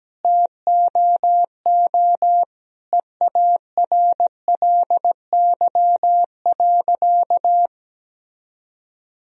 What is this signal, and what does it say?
Morse code, heard as a keyed tone, about 17 wpm, 700 Hz. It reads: TOO EARLY.
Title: Sentences from Top 1000 Words - Review Part 2 - 17wpm